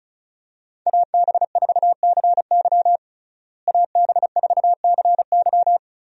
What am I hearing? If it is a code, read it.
Morse code, heard as a keyed tone, about 36 wpm, 700 Hz. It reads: AB4CY AB4CY
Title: USA Call Signs - 5 Characters - Repeated 2x - 35wpm